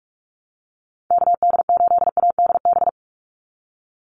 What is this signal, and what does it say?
Morse code, heard as a keyed tone, about 44 wpm, 700 Hz. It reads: KD8ADB
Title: USA Call Signs 45wpm